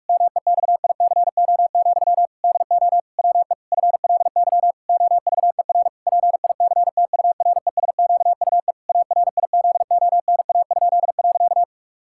Morse code, heard as a keyed tone, about 46 wpm, 700 Hz. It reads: MEXICO, DO WE FLY OVER PICTURESQUE ARIZONA?.
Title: Pangram Challenge v2 - 45wpm